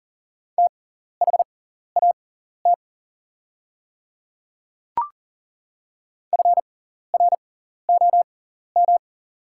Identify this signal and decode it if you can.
Morse code, heard as a keyed tone, about 40 wpm, spaced as for 14 wpm, 690 Hz. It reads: THAT FROM
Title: 6x Character Spacing - Top 100 Words - 40wpm